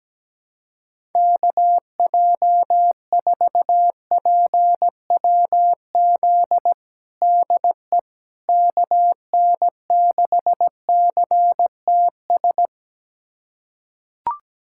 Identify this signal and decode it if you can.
Morse code, heard as a keyed tone, about 17 wpm, 700 Hz. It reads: KJ4PWZ DE KN6CTS
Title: USA Callsign DE Callsign 17wpm